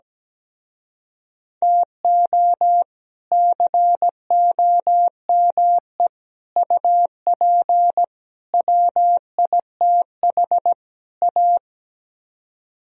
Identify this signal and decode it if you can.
Morse code, heard as a keyed tone, about 17 wpm, 700 Hz. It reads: TO COME UP WITH A
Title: Speed-Racing - Sets of 5 Words 17wpm